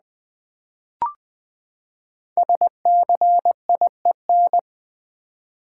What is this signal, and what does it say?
Morse code, heard as a keyed tone, about 20 wpm, 700 Hz. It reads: SCIEN
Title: Build A Word - 5 Letters At A Time - 10-Letter Words - 20wpm